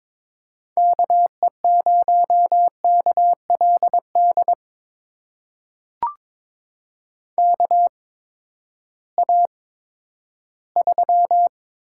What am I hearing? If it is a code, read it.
Morse code, heard as a keyed tone, about 22 wpm, spaced as for 13 wpm, 700 Hz. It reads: KE0KLD KA3
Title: USA Call Signs - 6 Characters - 8x Character Spacing - 22wpm